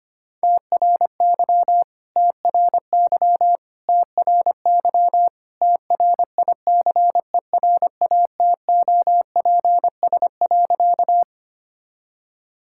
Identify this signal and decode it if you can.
Morse code, heard as a keyed tone, about 25 wpm, 700 Hz. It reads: TRY TRY TRY TRICERATOPS.
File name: practice / 25wpm / All-Age Appropriate Jokes - 25wpm